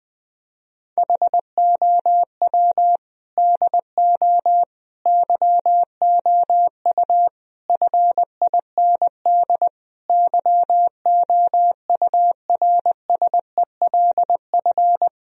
Morse code, heard as a keyed tone, about 20 wpm, 700 Hz. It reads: HOW DO YOU FIND YOURSELF
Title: Sentences from Top 1000 Words - Review Part 4 - 20wpm